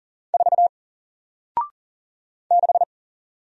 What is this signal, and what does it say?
Morse code, heard as a keyed tone, about 41 wpm, 700 Hz. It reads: 4 6
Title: Single Letter-Number 40wpm